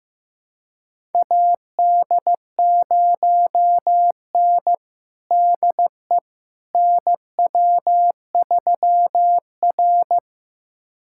Morse code, heard as a keyed tone, about 15 wpm, 700 Hz. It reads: AD0N DE NW3R
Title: USA Callsign DE Callsign - 4 Characters - 15wpm